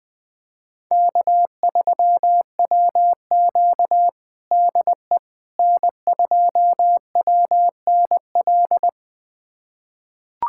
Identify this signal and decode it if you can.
Morse code, heard as a keyed tone, about 20 wpm, 700 Hz. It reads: K3WQ DE N2WNL
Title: USA Callsign DE Callsign 20wpm